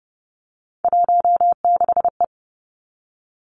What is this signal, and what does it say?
Morse code, heard as a keyed tone, about 30 wpm, 700 Hz. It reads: 16E
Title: Three Challenging Letter-Number Combinations 30wpm